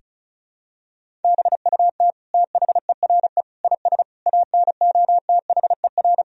Morse code, heard as a keyed tone, about 36 wpm, 700 Hz. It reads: BUT THERE IS ANOTHER
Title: Sentences from Top 1000 Words - Review Part 4 - 35wpm